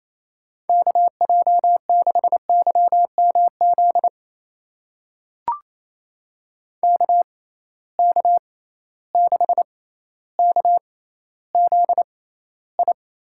Morse code, heard as a keyed tone, about 28 wpm, spaced as for 14 wpm, 700 Hz. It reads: KJ6YMZ KK6KZI
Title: USA Call Signs - 6 Characters - 6x Character Spacing - 28wpm